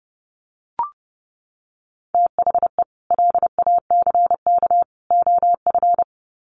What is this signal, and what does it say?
Morse code, heard as a keyed tone, about 30 wpm, 700 Hz. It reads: THE LACK OF
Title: Sets of 3 Words 30wpm